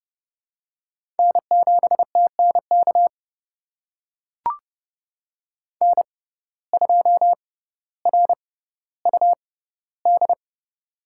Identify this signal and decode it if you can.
Morse code, heard as a keyed tone, about 30 wpm, spaced as for 15 wpm, 700 Hz. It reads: N7TNK N2RUD
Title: USA Call Signs - 5 Characters - 6x Character Spacing - 30wpm